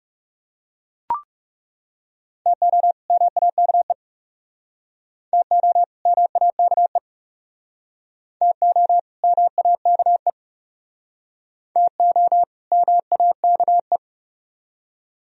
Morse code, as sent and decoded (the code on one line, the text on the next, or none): - --- / -- .- -.- . / - --- / -- .- -.- . / - --- / -- .- -.- . / - --- / -- .- -.- .
TO MAKE TO MAKE TO MAKE TO MAKE